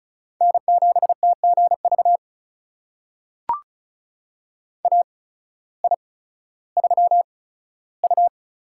Morse code, heard as a keyed tone, about 36 wpm, spaced as for 16 wpm, 700 Hz. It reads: N7TGV AI3U
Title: USA Call Signs - 8x Character Spacing - 35wpm